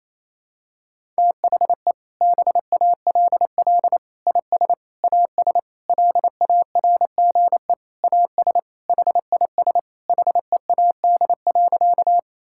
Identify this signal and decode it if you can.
Morse code, heard as a keyed tone, about 28 wpm, 700 Hz. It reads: THE BALL IS AS LARGE AS HIS HEAD.
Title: Sentences from Top 1000 Words - Review Part 6 - 28wpm